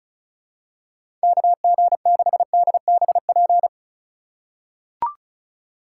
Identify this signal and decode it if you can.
Morse code, heard as a keyed tone, about 34 wpm, 700 Hz. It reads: KG6DBP
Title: USA Call Signs - 2x Character Spacing - 35wpm